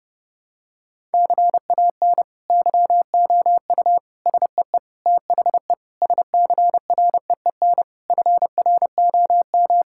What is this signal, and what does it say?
Morse code, heard as a keyed tone, about 30 wpm, 700 Hz. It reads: CAN YOU SEE THE SCREEN FROM
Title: Sentences from Top 1000 Words - Review Part 6 - 30wpm